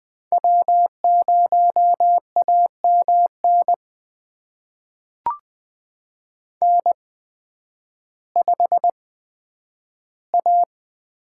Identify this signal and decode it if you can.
Morse code, heard as a keyed tone, about 20 wpm, spaced as for 11 wpm, 700 Hz. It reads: W0AMN N5A
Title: USA Call Signs - 8x Character Spacing - 20wpm